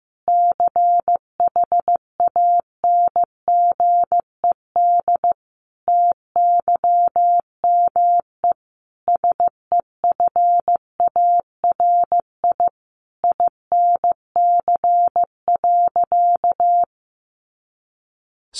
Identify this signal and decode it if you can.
Morse code, heard as a keyed tone, about 15 wpm, 700 Hz. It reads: CHANGED TYME SEFARI INC.
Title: A Sound Of Thunder 15wpm